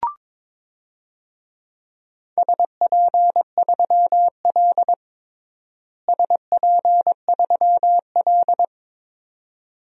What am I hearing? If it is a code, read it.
Morse code, heard as a keyed tone, about 22 wpm, 700 Hz. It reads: SP3L SP3L
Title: DXCC-100 Call Signs - 4 Characters - Repeated 2x - 22wpm